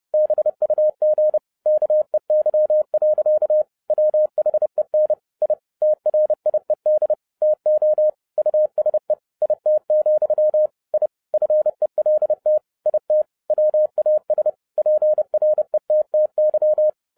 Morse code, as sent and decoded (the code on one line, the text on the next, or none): -... ..- --. / -.- . -.-- .-.-.- / .-- .... -. / .. / - .-. .. -.. / - --- / ..- ... / .. - --..-- / .. / ..-. .-.. - / .. - / .-- .- ... / .--. .-. - - -.--
BUG KEY. WHEN I TRIED TO USE IT, I FELT IT WAS PRETTY